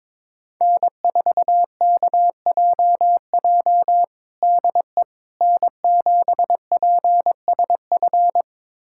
Morse code, heard as a keyed tone, about 22 wpm, 700 Hz. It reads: N4KJJ DE N7PSF
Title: USA Callsign DE Callsign - 5 Characters - 22wpm